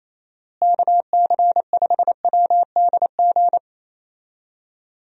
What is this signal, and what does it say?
Morse code, heard as a keyed tone, about 27 wpm, 700 Hz. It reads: KC5WDG
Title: USA Call Signs - 6 Characters - Repeated 3x - 28wpm